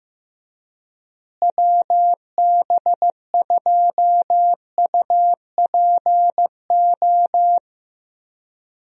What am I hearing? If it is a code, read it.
Morse code, heard as a keyed tone, about 15 wpm, 700 Hz. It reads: WB2UPO
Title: USA Call Signs - 6 Characters - Repeated 3x - 15wpm